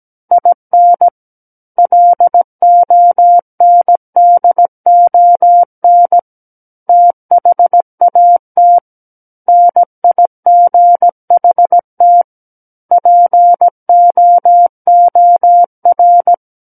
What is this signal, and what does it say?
Morse code, heard as a keyed tone, about 17 wpm, 700 Hz. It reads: IN LONDON THAT NIGHT POOR
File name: Book One / 08-Chapter 8 - 17 WPM